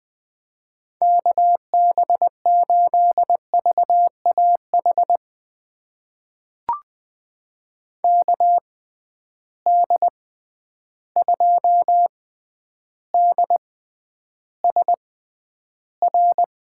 Morse code, heard as a keyed tone, about 20 wpm, spaced as for 10 wpm, 700 Hz. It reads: KB8VAH KD2DSR
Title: USA Call Signs - 6 Characters - 6x Character Spacing - 20wpm